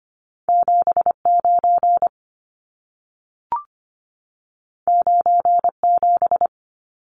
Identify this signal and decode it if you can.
Morse code, heard as a keyed tone, about 25 wpm, 700 Hz. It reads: 79 97
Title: Numbers - 2-Digit 25wpm